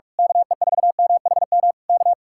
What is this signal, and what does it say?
Morse code, heard as a keyed tone, about 41 wpm, 700 Hz. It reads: KE4MHM K
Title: Speed-Racing Common QSO Phrases - 45wpm